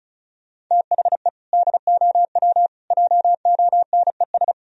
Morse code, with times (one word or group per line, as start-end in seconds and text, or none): 0.7-1.3 THE
1.5-2.7 DOW
2.9-4.5 JONES